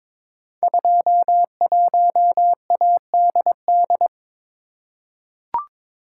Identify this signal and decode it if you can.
Morse code, heard as a keyed tone, about 22 wpm, 700 Hz. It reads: 21ADD
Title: Code Groups - 5 Letter:Numbers - 22wpm